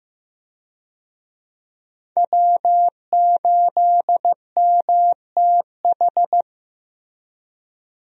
Morse code, heard as a keyed tone, about 15 wpm, 700 Hz. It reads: W8MTH
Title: USA Call Signs - 5 Characters - Repeated 2x - 15wpm